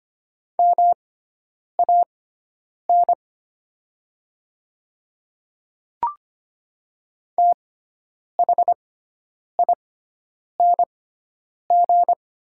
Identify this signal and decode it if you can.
Morse code, heard as a keyed tone, about 25 wpm, spaced as for 9 wpm, 700 Hz. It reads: MAN THING
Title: 6x Character Spacing - Top 200 Words - 25wpm